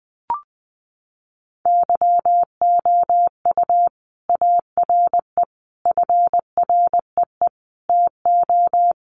Morse code, heard as a keyed tone, about 20 wpm, 700 Hz. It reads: YOU ARE FREE TO
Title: Sentences from Top 700 Words 20wpm